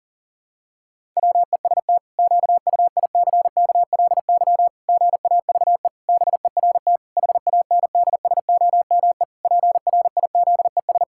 Morse code, heard as a keyed tone, about 40 wpm, 700 Hz. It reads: WEST QUICKLY GAVE BERT HANDSOME PRIZES
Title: Pangram Challenge v2 - 40wpm